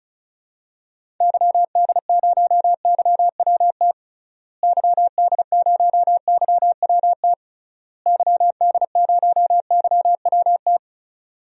Morse code, heard as a keyed tone, about 34 wpm, 700 Hz. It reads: YD0YWT YD0YWT YD0YWT
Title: DXCC-100 Call Signs - 6 Characters - Repeated 3x - 35wpm